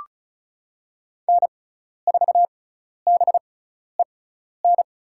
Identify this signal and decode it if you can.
Morse code, heard as a keyed tone, about 36 wpm, spaced as for 12 wpm, 700 Hz. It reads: N4BEN
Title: USA Call Signs - 5 Characters - 6x Character Spacing - 35wpm